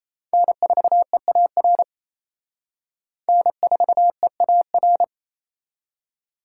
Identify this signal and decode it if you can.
Morse code, heard as a keyed tone, about 31 wpm, 700 Hz. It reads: N4EAR N4EAR
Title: Speed Racing - USA 1x3 Call Signs 28wpm